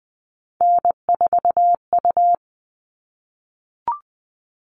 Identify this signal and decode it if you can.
Morse code, heard as a keyed tone, about 20 wpm, 700 Hz. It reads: N4U
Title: All Cut Numbers - 3-Digit 20wpm